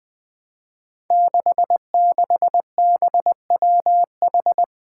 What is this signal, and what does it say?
Morse code, heard as a keyed tone, about 20 wpm, 700 Hz. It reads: 66BWH
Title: Five Challenging Letter-Number Combinations 20wpm